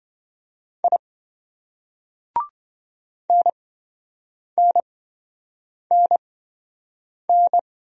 Morse code, heard as a keyed tone, about 26 wpm, spaced as for 7 wpm, 700 Hz. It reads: I NNNN